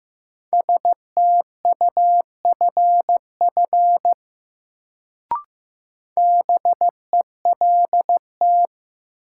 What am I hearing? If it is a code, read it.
Morse code, heard as a keyed tone, about 15 wpm, 700 Hz. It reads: STUFF BELT
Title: Sentences from Top 1400 Words - New Word Drill - 15wpm